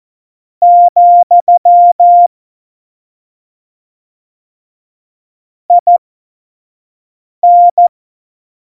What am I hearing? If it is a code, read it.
Morse code, heard as a keyed tone, about 14 wpm, spaced as for 5 wpm, 700 Hz. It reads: , IN
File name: Morsekurs / Witze / Witz 29. September 2020, 25 F70